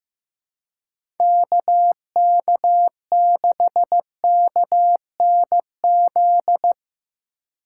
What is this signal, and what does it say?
Morse code, heard as a keyed tone, about 15 wpm, 700 Hz. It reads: KK6KNZ
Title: USA Call Signs - 6 Characters - 4x Character Spacing - 15wpm